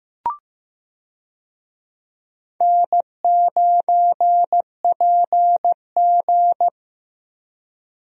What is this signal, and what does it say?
Morse code, heard as a keyed tone, about 15 wpm, 700 Hz. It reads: N9PG
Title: USA Call Signs - 4 Characters - Repeated 2x - 15wpm